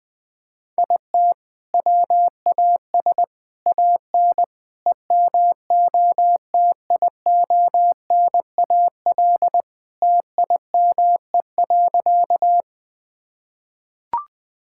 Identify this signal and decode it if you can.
Morse code, heard as a keyed tone, about 21 wpm, 700 Hz. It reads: IT WAS AN EMOTIONAL TIME.
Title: Sentences from Top 1000 Words - Review Part 5 - 20wpm